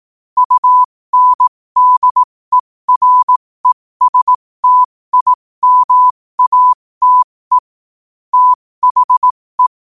Morse code, heard as a keyed tone, about 18 wpm, spaced as for 15 wpm, 1000 Hz. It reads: UNDERESTIMATE THE
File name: qotdfiles / 15wpm